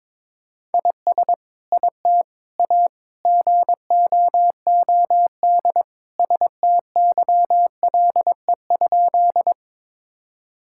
Morse code, heard as a keyed tone, about 22 wpm, 700 Hz. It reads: IS IT A GOOD STYLE?
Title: Sentences from Top 1000 Words - Review Part 5 - 22wpm